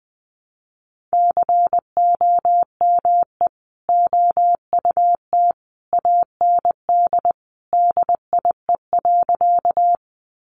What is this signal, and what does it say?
Morse code, heard as a keyed tone, about 20 wpm, 700 Hz. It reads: COME OUT AND DIE.
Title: Sentences from Top 1300 Words 20wpm